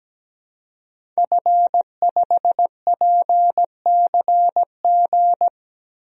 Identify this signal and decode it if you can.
Morse code, heard as a keyed tone, about 17 wpm, 700 Hz. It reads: F5PCG
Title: DXCC-100 Call Signs - 5 Characters - 17wpm